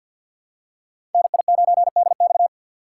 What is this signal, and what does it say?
Morse code, heard as a keyed tone, about 54 wpm, 700 Hz. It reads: NI9DX
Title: USA Call Signs - 5 Characters - 4x Character Spacing - 50wpm